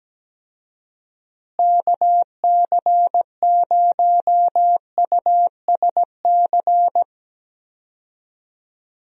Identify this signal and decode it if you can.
Morse code, heard as a keyed tone, about 17 wpm, 700 Hz. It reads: KC0USC
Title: USA Call Signs - 6 Characters - Repeated 3x - 17wpm